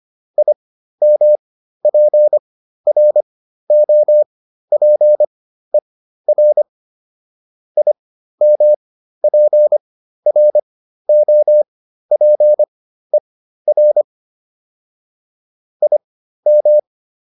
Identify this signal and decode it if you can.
Morse code, heard as a keyed tone, about 25 wpm, spaced as for 13 wpm, 600 Hz. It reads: IMPROPER IMPROPER IM